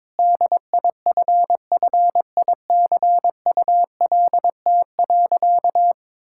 Morse code, heard as a keyed tone, about 21 wpm, 700 Hz. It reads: DIFFICULT.
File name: Sentences with 7-Word Phrases - 22wpm